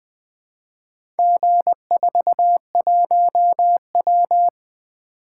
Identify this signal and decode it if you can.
Morse code, heard as a keyed tone, about 20 wpm, 700 Hz. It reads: G41W
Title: Four Letter-Number Combinations - 20wpm